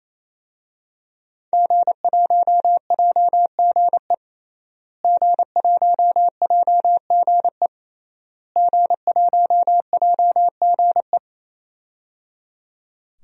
Morse code, heard as a keyed tone, about 28 wpm, 700 Hz. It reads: G1JGE G1JGE G1JGE
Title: DXCC-100 Call Signs - 5 Characters - Repeated 3x - 28wpm